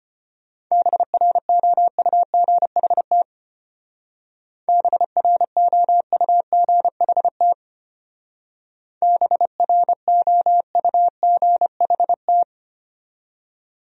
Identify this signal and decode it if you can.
Morse code, heard as a keyed tone, about 31 wpm, 700 Hz. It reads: BROUGHT BROUGHT BROUGHT
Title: Speed-Racing Top 500 Words 25wpm